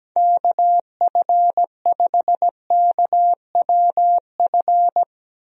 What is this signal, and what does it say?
Morse code, heard as a keyed tone, about 17 wpm, 700 Hz. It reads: KF5KWF
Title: USA Call Signs - 6 Characters - Repeated 2x - 17wpm